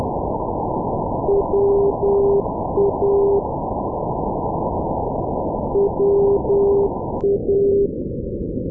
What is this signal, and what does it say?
Morse code, heard as a keyed tone, about 10 wpm, 400 Hz, with noise. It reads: WA WA
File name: europe / s-230 ndb stockholm-arlanda wa-348 sa0cam-2020-12-kiwi regional-rb